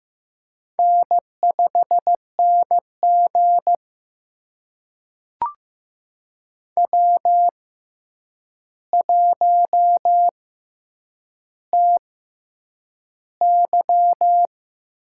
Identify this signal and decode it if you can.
Morse code, heard as a keyed tone, about 15 wpm, spaced as for 8 wpm, 700 Hz. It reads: N5NG W1TY
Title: USA Call Signs - 4 Characters - 6x Character Spacing - 15wpm